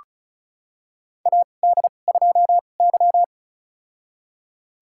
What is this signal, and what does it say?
Morse code, heard as a keyed tone, about 34 wpm, spaced as for 25 wpm, 700 Hz. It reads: AD2Y
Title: USA Call Signs - 2x Character Spacing - 35wpm